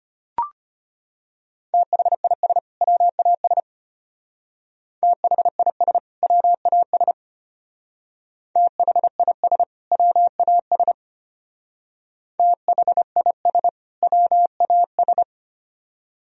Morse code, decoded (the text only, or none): THIS WAS THIS WAS THIS WAS THIS WAS